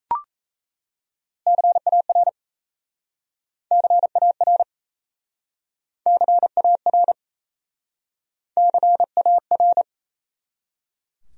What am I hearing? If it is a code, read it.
Morse code, heard as a keyed tone, about 38 wpm, 700 Hz. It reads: CAR CAR CAR CAR